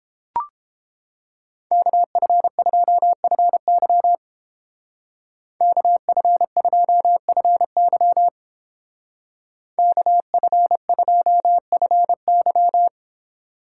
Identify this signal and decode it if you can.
Morse code, heard as a keyed tone, about 30 wpm, 700 Hz. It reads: KF2FY KF2FY KF2FY